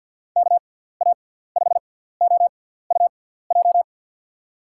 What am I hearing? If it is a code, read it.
Morse code, heard as a keyed tone, about 53 wpm, spaced as for 17 wpm, 700 Hz. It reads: KA5OUJ
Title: USA Call Signs - 6x Character Spacing - 50wpm